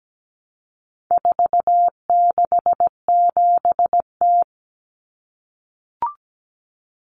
Morse code, heard as a keyed tone, about 17 wpm, 700 Hz. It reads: 467T
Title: Common Cut Numbers - 4-Digit 17wpm